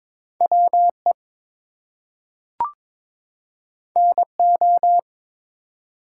Morse code, heard as a keyed tone, about 22 wpm, 700 Hz. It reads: WE NO
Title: Two Letter Words - 22wpm